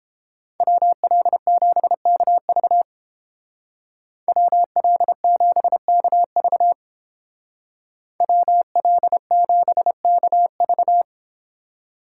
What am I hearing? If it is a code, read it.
Morse code, heard as a keyed tone, about 28 wpm, 700 Hz. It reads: WL7KV WL7KV WL7KV